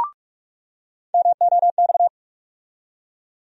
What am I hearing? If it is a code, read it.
Morse code, heard as a keyed tone, about 45 wpm, 700 Hz. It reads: MOX
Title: USA Call Sign Postfixes - 3 Letters 45wpm